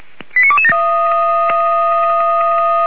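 Five-tone selective call (ZVEI), heard as quick runs of five tones, tones of about 80 ms. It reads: 89227
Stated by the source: ZVEI five-tone call 89138 received on an SDR, channel 464, from Feuerwehr Orlamünde.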